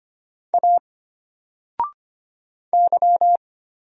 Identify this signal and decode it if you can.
Morse code, heard as a keyed tone, about 25 wpm, 700 Hz. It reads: A Y
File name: Single-Letter-Number-25wpm